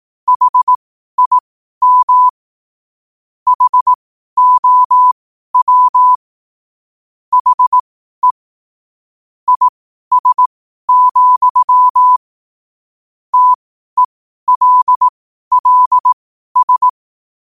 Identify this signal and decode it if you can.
Morse code, heard as a keyed tone, about 18 wpm, spaced as for 12 wpm, 1000 Hz. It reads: HIM HOW HE IS, TELLS
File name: qotdfiles / 12wpm